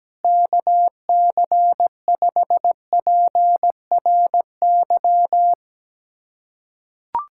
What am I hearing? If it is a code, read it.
Morse code, heard as a keyed tone, about 17 wpm, 700 Hz. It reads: KC5PRY
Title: USA Callsign DE Callsign - 6 Characters - 17wpm